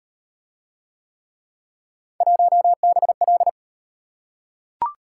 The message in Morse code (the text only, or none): 1BL